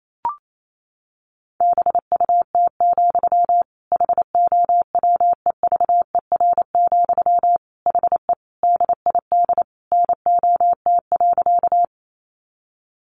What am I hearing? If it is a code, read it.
Morse code, heard as a keyed tone, about 28 wpm, 700 Hz. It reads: BUT, HOWEVER, HE DID NOT.
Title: Sentences from Top 600 Words 28wpm